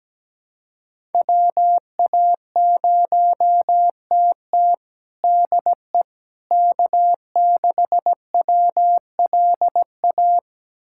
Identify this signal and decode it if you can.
Morse code, heard as a keyed tone, about 17 wpm, 700 Hz. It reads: WA0TT DE K6WLA
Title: USA Callsign DE Callsign - 5 Characters - 17wpm